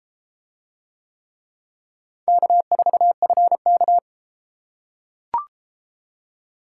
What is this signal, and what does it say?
Morse code, heard as a keyed tone, about 32 wpm, 700 Hz. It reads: K4FK